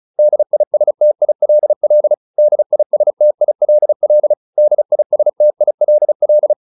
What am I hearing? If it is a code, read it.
Morse code, heard as a keyed tone, about 34 wpm, 600 Hz. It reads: DISTILL DISTILL DISTILL